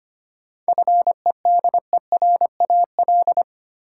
Morse code, heard as a keyed tone, about 25 wpm, 700 Hz. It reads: FEDERAL
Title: Sentences from Top 1000 Words - Review Part 5 - New Word Drill - 25wpm